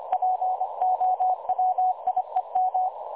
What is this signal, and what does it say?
Morse code, heard as a keyed tone, about 25 wpm, 740 Hz. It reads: POWIEM
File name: dlazaawansowanych / nagraniazpasma / websdr recording 3565.0kHz